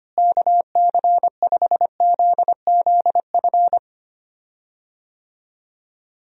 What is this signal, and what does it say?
Morse code, heard as a keyed tone, about 25 wpm, 700 Hz. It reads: KC5ZZF
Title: USA Call Signs - 6 Characters - Repeated 3x - 25wpm